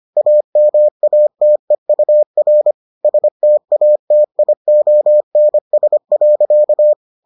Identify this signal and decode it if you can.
Morse code, heard as a keyed tone, about 25 wpm, 600 Hz. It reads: AMATEUR STATIONS.